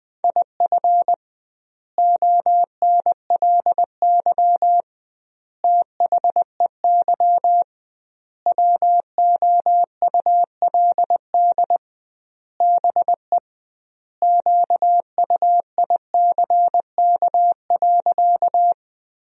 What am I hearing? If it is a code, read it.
Morse code, heard as a keyed tone, about 21 wpm, 700 Hz. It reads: IF ONLY THEY WOULD BE QUICK.
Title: 2x Word-Spacing - Sentences from Top 500 Words - 20wpm